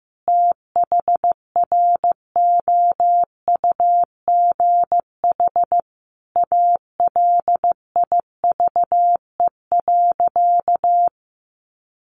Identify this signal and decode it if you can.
Morse code, heard as a keyed tone, about 15 wpm, 700 Hz. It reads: THROUGH ALIVE.
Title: A Sound Of Thunder 15wpm